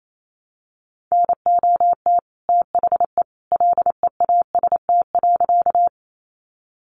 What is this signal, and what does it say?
Morse code, heard as a keyed tone, about 28 wpm, 700 Hz. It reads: NOT THE LEAST.
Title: Sentences from Top 700 Words 28wpm